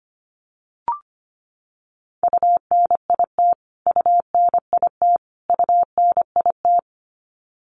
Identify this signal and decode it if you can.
Morse code, heard as a keyed tone, about 25 wpm, 700 Hz. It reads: UNIT UNIT UNIT
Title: Top 500 Words Repeated 25wpm